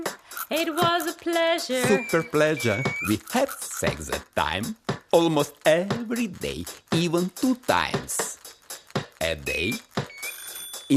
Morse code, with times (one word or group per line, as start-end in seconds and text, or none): none